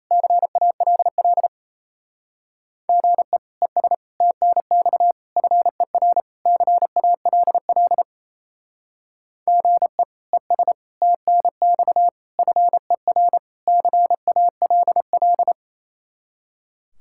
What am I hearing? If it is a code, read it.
Morse code, heard as a keyed tone, about 34 wpm, 700 Hz. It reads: CALL GE ES TNX FER CALL GE ES TNX FER CALL
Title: Speed-Racing Common QSO Phrases - 28wpm